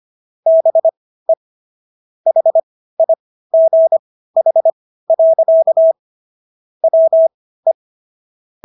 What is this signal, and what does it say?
Morse code, heard as a keyed tone, about 25 wpm, spaced as for 15 wpm, 650 Hz. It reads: BE HIGH. WE